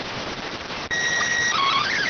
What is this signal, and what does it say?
SSTV (slow-scan television), PD 290.